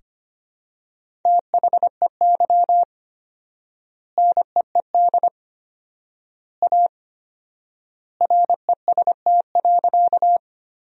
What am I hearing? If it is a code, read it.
Morse code, heard as a keyed tone, about 25 wpm, 700 Hz. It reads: THEY NEED A REST.
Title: 4x Word-Spacing - Sentences from Top 500 Words - 25wpm